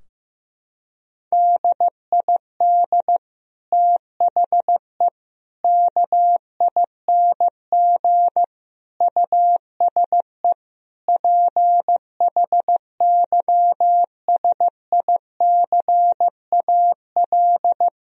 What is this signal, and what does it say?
Morse code, heard as a keyed tone, about 15 wpm, 700 Hz. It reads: DID THE KING USE PHYSICAL
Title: Sentences from Top 1000 Words - Review Part 4 - 15wpm